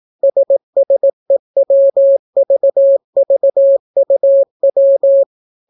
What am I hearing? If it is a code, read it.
Morse code, heard as a keyed tone, about 18 wpm, 550 Hz. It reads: SSEWVVUW